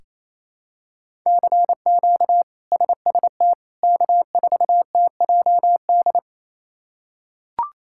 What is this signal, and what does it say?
Morse code, heard as a keyed tone, about 28 wpm, 700 Hz. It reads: CQ SST K4TJD
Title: K1USN SST QSO Practice - 28wpm